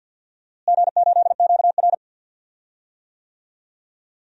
Morse code, heard as a keyed tone, about 52 wpm, 700 Hz. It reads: G8QR
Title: DXCC-100 Call Signs - 4 Characters - Repeated 3x - 50wpm